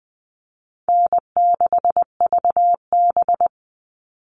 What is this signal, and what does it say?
Morse code, heard as a keyed tone, about 20 wpm, 700 Hz. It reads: N6VB
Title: All Cut Numbers - 4-Digit 20wpm